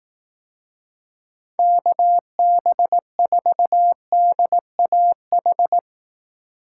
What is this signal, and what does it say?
Morse code, heard as a keyed tone, about 18 wpm, 700 Hz. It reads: KB4DAH
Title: Speed Racing - USA 2x3 Call Signs 15wpm